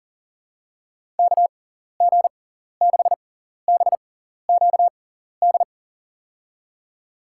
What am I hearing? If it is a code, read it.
Morse code, heard as a keyed tone, about 40 wpm, spaced as for 14 wpm, 700 Hz. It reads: KG6BQD
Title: USA Call Signs - 6x Character Spacing - 40wpm